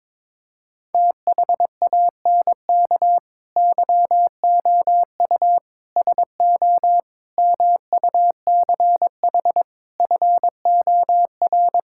Morse code, heard as a keyed tone, about 22 wpm, 700 Hz. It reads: THANK YOU SO MUCH FOR
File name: practice / 22wpm / Sentences with 4-Word Phrases - 22wpm